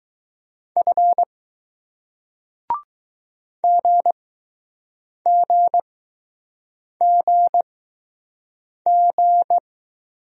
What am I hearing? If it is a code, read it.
Morse code, heard as a keyed tone, about 20 wpm, spaced as for 6 wpm, 700 Hz. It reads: F GGGG